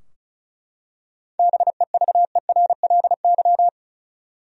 Morse code, heard as a keyed tone, about 36 wpm, 700 Hz. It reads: BEVERLY
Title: USA Common Names - 35wpm